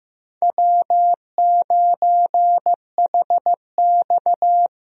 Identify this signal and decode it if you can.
Morse code, heard as a keyed tone, about 15 wpm, 700 Hz. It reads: W9HX
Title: USA Call Signs - 4 Characters - Repeated 2x - 15wpm